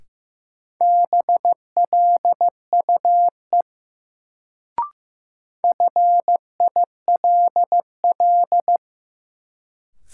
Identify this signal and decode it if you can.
Morse code, heard as a keyed tone, about 15 wpm, 700 Hz. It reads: BLUE FILL
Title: Sentences from Top 1000 Words - Review Part 3 - New Word Drill - 15wpm